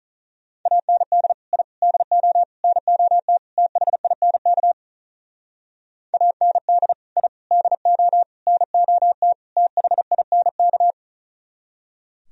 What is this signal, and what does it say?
Morse code, heard as a keyed tone, about 44 wpm, 700 Hz. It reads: AND I DO NOT THINK AND I DO NOT THINK